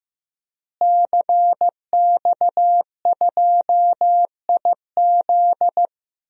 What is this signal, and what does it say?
Morse code, heard as a keyed tone, about 15 wpm, 700 Hz. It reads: CX2IZ
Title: DXCC-100 Call Signs - 5 Characters - Repeated 2x - 15wpm